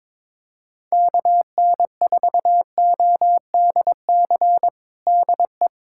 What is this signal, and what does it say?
Morse code, heard as a keyed tone, about 22 wpm, 700 Hz. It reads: KN4ODC DE
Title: USA Callsign DE Callsign 22wpm